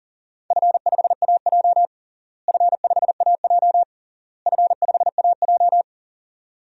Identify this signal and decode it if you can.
Morse code, heard as a keyed tone, about 40 wpm, 700 Hz. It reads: F5AJ F5AJ F5AJ
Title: DXCC-100 Call Signs - 4 Characters - Repeated 3x - 40wpm